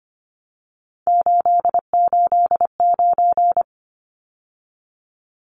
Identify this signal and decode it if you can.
Morse code, heard as a keyed tone, about 25 wpm, 690 Hz. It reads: 889
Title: Numbers - 3-Digit 25wpm